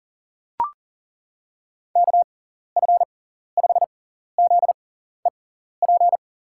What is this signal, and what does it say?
Morse code, heard as a keyed tone, about 37 wpm, spaced as for 14 wpm, 700 Hz. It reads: KF5ZEP
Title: USA Call Signs - 6x Character Spacing - 40wpm